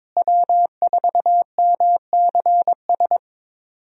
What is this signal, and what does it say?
Morse code, heard as a keyed tone, about 22 wpm, 700 Hz. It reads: W4MCS
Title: USA Call Signs - 5 Characters - Repeated 3x - 22wpm